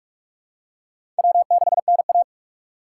Morse code, heard as a keyed tone, about 46 wpm, 700 Hz. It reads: W6NA
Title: USA Call Signs - 4 Characters - 3x Character Spacing - 45wpm